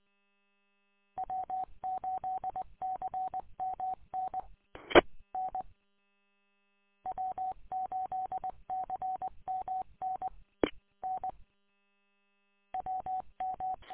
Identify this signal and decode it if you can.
Morse code, heard as a keyed tone, about 20 wpm, 750 Hz. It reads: W8CMN EN W8CMN N WM
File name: w8cmn- -combined